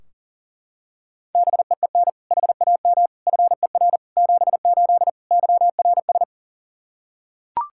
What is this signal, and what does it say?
Morse code, heard as a keyed tone, about 40 wpm, 700 Hz. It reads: BEEN HAM FER 78 YRS
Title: Practice QSO - Exchange 3 - More Info - 40wpm